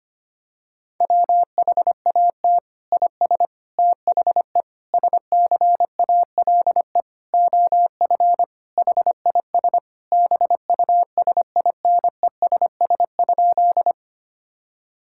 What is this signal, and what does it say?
Morse code, heard as a keyed tone, about 25 wpm, 700 Hz. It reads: WHAT IS THE SCALE OF HIS BUSINESS?
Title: Sentences from Top 1000 Words - Review Part 5 - 25wpm